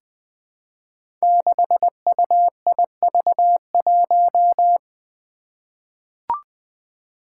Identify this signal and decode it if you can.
Morse code, heard as a keyed tone, about 20 wpm, 700 Hz. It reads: 6UIV1
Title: Code Groups - 5 Letter:Numbers - 20wpm